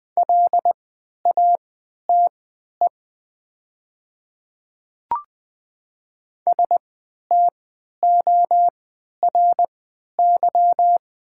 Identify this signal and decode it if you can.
Morse code, heard as a keyed tone, about 20 wpm, spaced as for 11 wpm, 700 Hz. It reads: LATE STORY
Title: 3x Character Spacing - Top 300 Words - 20wpm